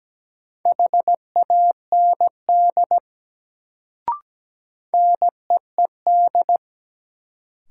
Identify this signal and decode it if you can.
Morse code, heard as a keyed tone, about 17 wpm, 700 Hz. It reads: HAND NEED
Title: Sentences from Top 200 Words - New Word Drill - 17wpm